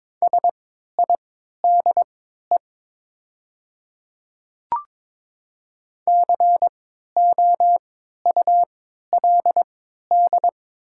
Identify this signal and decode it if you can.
Morse code, heard as a keyed tone, about 22 wpm, spaced as for 13 wpm, 700 Hz. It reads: SIDE COULD
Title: 3x Character Spacing - Top 100 Words - 22wpm